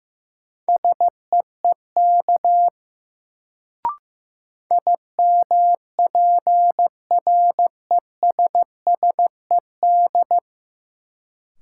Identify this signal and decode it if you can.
Morse code, heard as a keyed tone, about 15 wpm, 700 Hz. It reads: SEEK IMPRESSED
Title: Sentences from Top 1600 Words - New Word Drill Alt - 15wpm